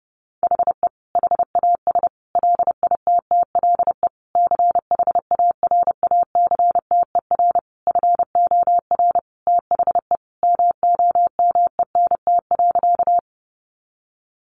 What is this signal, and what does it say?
Morse code, heard as a keyed tone, about 30 wpm, 700 Hz. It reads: HE HAS LITTLE CHARACTER FOR THE MOMENT.
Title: Sentences from Top 1200 Words 30wpm